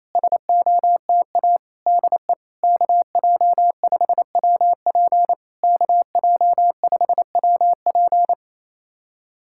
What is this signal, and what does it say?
Morse code, heard as a keyed tone, about 28 wpm, 700 Hz. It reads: SOTA DE KJ5WP KJ5WP
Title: SOTA QSO Practice 28wpm